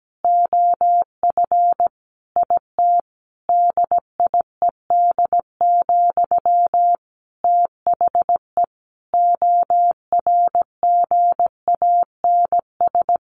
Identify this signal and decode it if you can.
Morse code, heard as a keyed tone, about 17 wpm, 700 Hz. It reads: OF IT DIED, THE ORGANS
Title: A Sound Of Thunder 17wpm